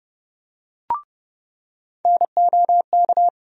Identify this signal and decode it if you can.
Morse code, heard as a keyed tone, about 32 wpm, 700 Hz. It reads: NOK
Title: USA Call Sign Postfixes - 3 Letters 30wpm